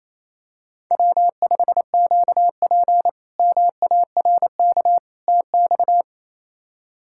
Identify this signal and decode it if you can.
Morse code, heard as a keyed tone, about 28 wpm, 700 Hz. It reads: W5QP MARK TX
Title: K1USN SST QSO Practice - 28wpm